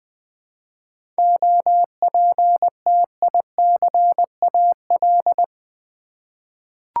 Morse code, heard as a keyed tone, about 20 wpm, 700 Hz. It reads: OPTICAL
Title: 7 Letter Words - First 3 Letters Given - 20wpm